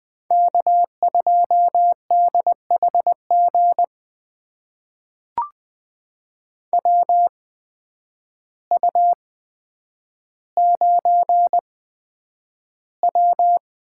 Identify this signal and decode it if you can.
Morse code, heard as a keyed tone, about 20 wpm, spaced as for 9 wpm, 700 Hz. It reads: K2DHG WU9W
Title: USA Call Signs - 8x Character Spacing - 20wpm